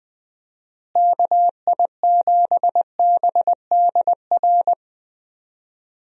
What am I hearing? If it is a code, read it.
Morse code, heard as a keyed tone, about 20 wpm, 700 Hz. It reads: KI7BDR
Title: USA Call Signs - 6 Characters - Repeated 3x - 20wpm